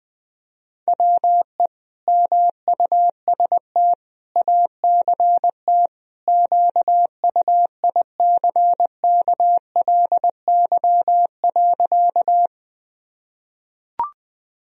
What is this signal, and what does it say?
Morse code, heard as a keyed tone, about 20 wpm, 700 Hz. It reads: WE MUST ACT QUICKLY.
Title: Sentences from Top 1000 Words - Review Part 1 - 20wpm